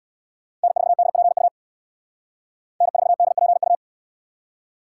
Morse code, heard as a keyed tone, about 77 wpm, 700 Hz. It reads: N4DPU N4DPU